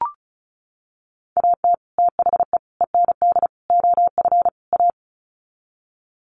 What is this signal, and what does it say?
Morse code, heard as a keyed tone, about 34 wpm, 700 Hz. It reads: AT THE END OF A
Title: Sets of 5 Words 35wpm